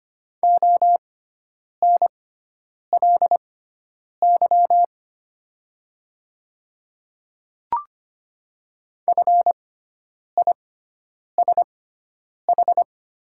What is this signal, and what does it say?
Morse code, heard as a keyed tone, about 25 wpm, spaced as for 9 wpm, 700 Hz. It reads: ONLY FISH
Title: 6x Character Spacing - Top 500 Words - 25wpm